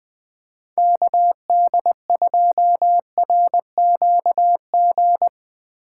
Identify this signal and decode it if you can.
Morse code, heard as a keyed tone, about 20 wpm, 700 Hz. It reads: KD2RQG